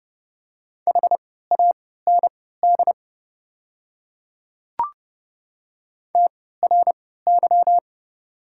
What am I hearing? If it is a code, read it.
Morse code, heard as a keyed tone, about 32 wpm, spaced as for 17 wpm, 700 Hz. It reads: HAND TRY
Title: 3x Character Spacing - Top 300 Words - 30wpm